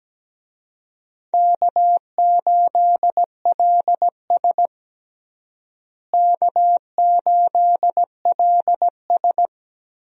Morse code, heard as a keyed tone, about 17 wpm, 700 Hz. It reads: K8LS K8LS
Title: USA Call Signs - 4 Characters - Repeated 2x - 17wpm